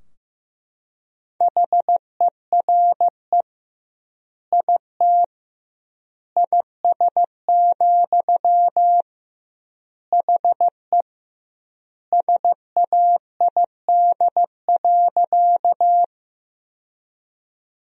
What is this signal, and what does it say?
Morse code, heard as a keyed tone, about 16 wpm, 700 Hz. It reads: HERE IT IS, HE SAID.
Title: 2x Word-Spacing - Sentences from Top 200 Words - 15wpm